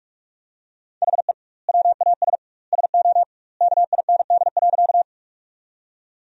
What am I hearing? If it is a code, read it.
Morse code, heard as a keyed tone, about 46 wpm, 700 Hz. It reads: HE WAS SO KIND.
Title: 2x Word-Spacing - Sentences from Top 200 Words - 45wpm